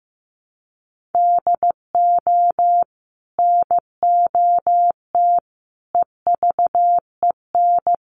Morse code, heard as a keyed tone, about 15 wpm, 700 Hz. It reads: DO NOT EVEN
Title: Sets of 5 Words Encore 15wpm